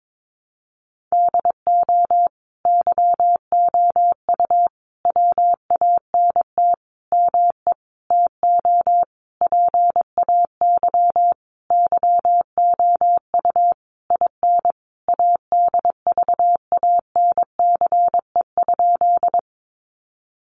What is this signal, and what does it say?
Morse code, heard as a keyed tone, about 22 wpm, 700 Hz. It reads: DO YOU WANT ME TO PAY YOU IN ADVANCE?
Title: Sentences from Top 1400 Words 22wpm